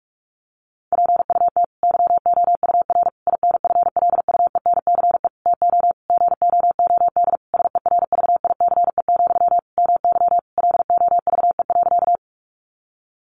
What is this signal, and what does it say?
Morse code, heard as a keyed tone, about 46 wpm, 700 Hz. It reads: PUT YOUR INFLUENCE TO GOOD SERVICE, MY LOVE.
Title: Sentences from Top 1100 Words 45wpm